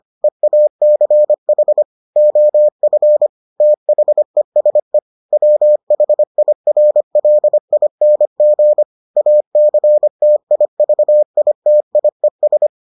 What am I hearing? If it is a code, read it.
Morse code, heard as a keyed tone, about 25 wpm, 600 Hz. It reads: EACH OF THESE WHIRLING ACTIVITIES